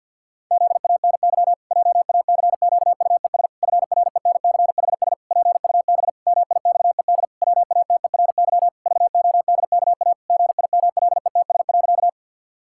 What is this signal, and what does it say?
Morse code, heard as a keyed tone, about 50 wpm, 700 Hz. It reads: ZANY JACQUES FRENCHS PUB MIXED WATERY VODKA GIMLETS.
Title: Pangram Challenge v2 - 50wpm